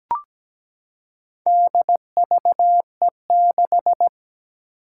Morse code, heard as a keyed tone, about 17 wpm, 700 Hz. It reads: DVE6
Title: Four Challenging Letter-Number Combinations 17wpm